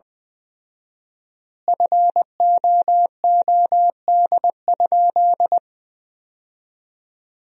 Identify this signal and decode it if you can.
Morse code, heard as a keyed tone, about 21 wpm, 700 Hz. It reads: FOOD?
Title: 4x Word-Spacing - Sentences from Top 300 Words - 20wpm